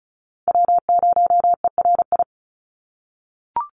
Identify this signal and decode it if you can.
Morse code, heard as a keyed tone, about 34 wpm, 700 Hz. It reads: W0ERI
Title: NA SOTA Chaser Callsigns 35wpm